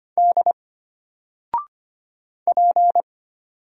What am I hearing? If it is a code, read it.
Morse code, heard as a keyed tone, about 25 wpm, 700 Hz. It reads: D P
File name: Single-Letter-Number-25wpm